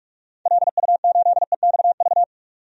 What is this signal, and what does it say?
Morse code, heard as a keyed tone, about 47 wpm, 700 Hz. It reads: LU8EXV
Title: DXCC-100 Call Signs - 6 Characters - 45wpm